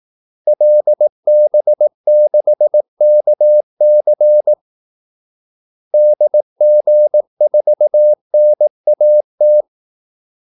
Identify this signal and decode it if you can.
Morse code, heard as a keyed tone, about 18 wpm, 600 Hz. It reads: LB6KC DG4NAT
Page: Kaldesignaler 18 wpm